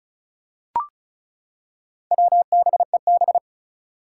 Morse code, heard as a keyed tone, about 34 wpm, 700 Hz. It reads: WBEB
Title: Four Challenging Letter-Number Combinations 35wpm